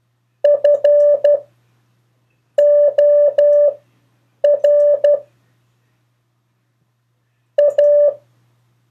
Morse code, heard as a keyed tone, about 12 wpm, spaced as for 6 wpm, 580 Hz. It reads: FOR A
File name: 4of4-sentences-mistakes-test